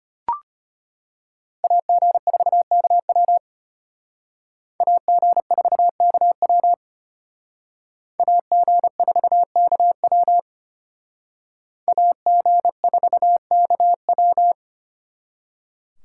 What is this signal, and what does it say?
Morse code, heard as a keyed tone, about 33 wpm, 700 Hz. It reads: AG4KW AG4KW AG4KW AG4KW